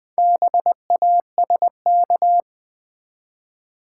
Morse code, heard as a keyed tone, about 20 wpm, 700 Hz. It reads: BASK
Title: Build A Word - 4 Letters At A Time - 10-Letter Words - 20wpm